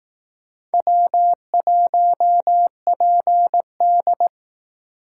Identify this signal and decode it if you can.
Morse code, heard as a keyed tone, about 18 wpm, 700 Hz. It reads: W1PD
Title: Speed Racing - USA 1x2 Call Signs 15wpm